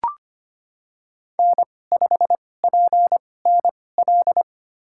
Morse code, heard as a keyed tone, about 25 wpm, spaced as for 18 wpm, 700 Hz. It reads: N5PNL
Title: USA Call Signs - 2x Character Spacing - 25wpm